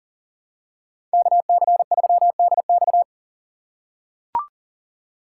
Morse code, heard as a keyed tone, about 38 wpm, 700 Hz. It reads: KC3DX
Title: USA Call Signs - 5 Characters - 4x Character Spacing - 40wpm